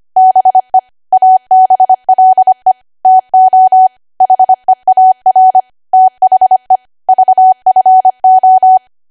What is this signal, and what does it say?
Morse code, heard as a keyed tone, about 25 wpm, 750 Hz. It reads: BE ABLE TO HEAR THE VFO
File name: Archive / 25 WPM / 25WPM